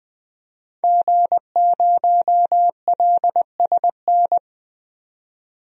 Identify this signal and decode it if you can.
Morse code, heard as a keyed tone, about 20 wpm, 700 Hz. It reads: G0LSN
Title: DXCC-100 Call Signs - 5 Characters - 20wpm